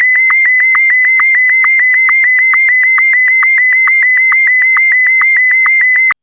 SSTV (slow-scan television), Martin 1.
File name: index06 / segnalidigitali / SSTV M1